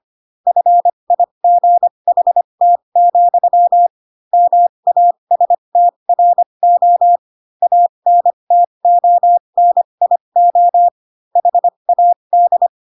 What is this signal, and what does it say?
Morse code, heard as a keyed tone, about 25 wpm, spaced as for 22 wpm, 700 Hz. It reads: FIGHT, MASTRO ANTONIO HAD
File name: Pinocchio / Pinocchio ch2 18 25 ik1hge